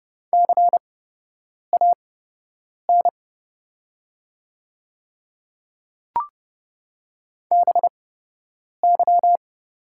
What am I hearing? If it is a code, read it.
Morse code, heard as a keyed tone, about 30 wpm, spaced as for 8 wpm, 700 Hz. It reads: CAN BY
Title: 8x Character Spacing - Top 100 Words - 30wpm